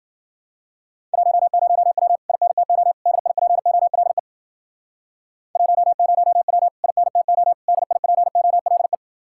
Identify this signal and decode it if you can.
Morse code, heard as a keyed tone, about 52 wpm, 700 Hz. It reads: 10W INTO DIPOLE 10W INTO DIPOLE